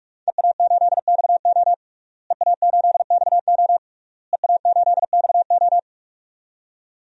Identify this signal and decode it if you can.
Morse code, heard as a keyed tone, about 46 wpm, 700 Hz. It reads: EA8XO EA8XO EA8XO
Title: DXCC-100 Call Signs - 5 Characters - Repeated 3x - 45wpm